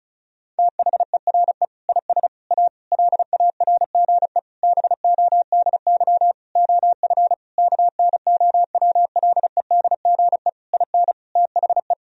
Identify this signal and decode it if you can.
Morse code, heard as a keyed tone, about 36 wpm, 700 Hz. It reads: THERE IS A LARGE BODY OF KNOWLEDGE IN THE
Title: Sentences from Top 1000 Words - Review Part 4 - 35wpm